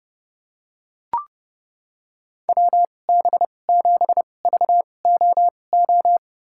Morse code, heard as a keyed tone, about 31 wpm, spaced as for 22 wpm, 700 Hz. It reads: WB7VOO
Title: USA Call Signs - 6 Characters - 2x Character Spacing - 30wpm